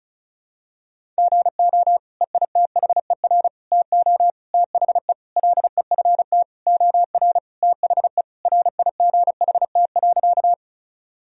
Morse code, heard as a keyed tone, about 36 wpm, 700 Hz. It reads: GO EITHER TO THE LEFT OR THE RIGHT.
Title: Sentences from Top 1000 Words - Review Part 4 - 35wpm